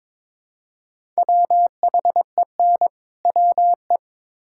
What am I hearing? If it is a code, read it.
Morse code, heard as a keyed tone, about 22 wpm, 700 Hz. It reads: WHEN WE
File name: Build A Phrase - 2-Word Phrases - 22wpm